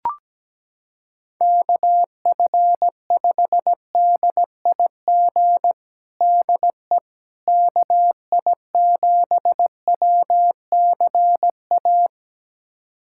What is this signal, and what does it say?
Morse code, heard as a keyed tone, about 17 wpm, 700 Hz. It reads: KF5DIG DE KI7WCA
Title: USA Callsign DE Callsign 17wpm